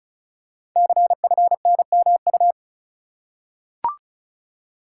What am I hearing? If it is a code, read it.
Morse code, heard as a keyed tone, about 36 wpm, 690 Hz. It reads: CFNMU
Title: Code Groups - 5 Letters - 35wpm